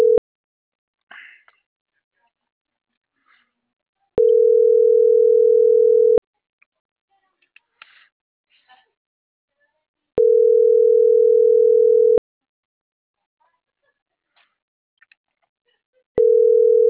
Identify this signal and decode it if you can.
Ringback tone (North America), 440 + 480 Hz.